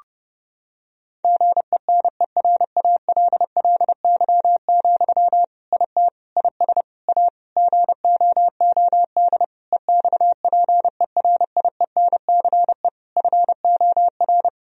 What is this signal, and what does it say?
Morse code, heard as a keyed tone, about 30 wpm, 700 Hz. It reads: GENERALLY, IT IS A GOOD EXPERIENCE FOR
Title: Sentences from Top 1000 Words - Review Part 4 - 30wpm